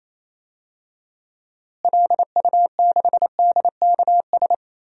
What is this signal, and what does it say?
Morse code, heard as a keyed tone, about 28 wpm, 700 Hz. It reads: LU6DKS
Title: DXCC-100 Call Signs - 6 Characters - Repeated 3x - 28wpm